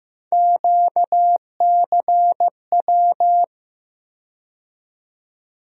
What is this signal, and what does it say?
Morse code, heard as a keyed tone, about 15 wpm, 700 Hz. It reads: QCW
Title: USA Call Sign Postfixes - 3 Letters 15wpm